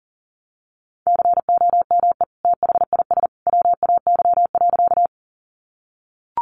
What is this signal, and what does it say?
Morse code, heard as a keyed tone, about 40 wpm, 700 Hz. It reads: COME THIS WAY.
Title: Sentences from Top 100 Words 40wpm